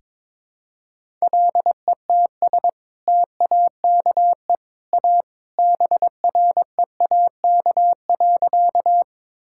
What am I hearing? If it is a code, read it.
Morse code, heard as a keyed tone, about 22 wpm, 700 Hz. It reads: LETS TAKE A BREAK.
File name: practice / 22wpm / Sentences with 2-Word Phrases - 22wpm